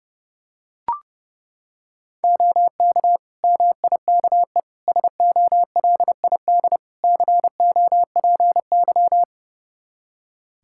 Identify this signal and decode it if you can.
Morse code, heard as a keyed tone, about 30 wpm, 700 Hz. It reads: OK MIKE SOLID COPY
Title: Practice QSO - Exchange 3 - More Info - 30wpm